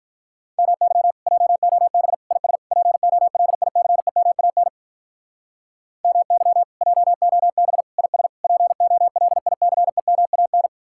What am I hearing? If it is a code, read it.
Morse code, heard as a keyed tone, about 45 wpm, 700 Hz. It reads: MY JOB IS POLICEMAN MY JOB IS POLICEMAN